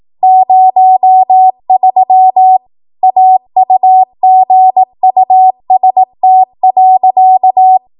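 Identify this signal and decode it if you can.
Morse code, heard as a keyed tone, about 18 wpm, 750 Hz. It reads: 03 AUGUST.
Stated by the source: 18 WPM Morse code practice from a national amateur radio association's code archive.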